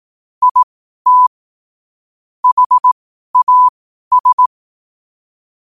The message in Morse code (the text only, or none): IT HAS